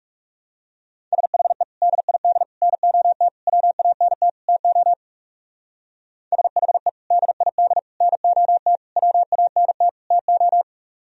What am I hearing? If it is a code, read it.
Morse code, heard as a keyed tone, about 37 wpm, 700 Hz. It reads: SHE DID NOT WANT TO SHE DID NOT WANT TO
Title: Speed-Racing - Sets of 5 Words 30wpm